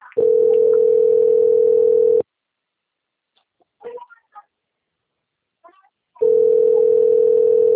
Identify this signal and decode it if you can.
Ringback tone (North America), 440 + 480 Hz.